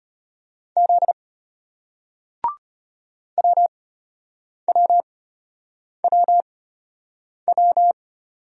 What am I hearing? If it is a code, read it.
Morse code, heard as a keyed tone, about 34 wpm, spaced as for 8 wpm, 700 Hz. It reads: Z WWWW